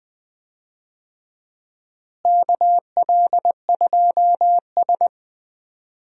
Morse code, heard as a keyed tone, about 20 wpm, 700 Hz. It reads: KL2S
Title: USA Call Signs - 4 Characters - Repeated 2x - 20wpm